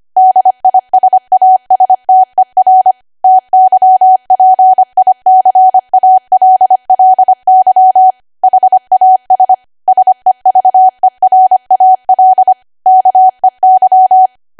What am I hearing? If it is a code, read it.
Morse code, heard as a keyed tone, about 25 wpm, 750 Hz. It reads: DISASTER TYPICALLY HAS SEVERAL KEY
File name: Archive / 25 WPM / 25WPM